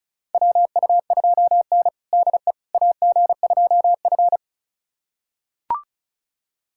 Morse code, heard as a keyed tone, about 36 wpm, 700 Hz. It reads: WU2N DE AG2F
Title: USA Callsign DE Callsign - 4 Characters - 35wpm